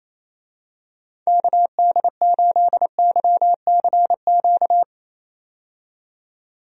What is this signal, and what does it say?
Morse code, heard as a keyed tone, about 28 wpm, 700 Hz. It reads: KD8YCQ
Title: USA Call Signs - 6 Characters - 28wpm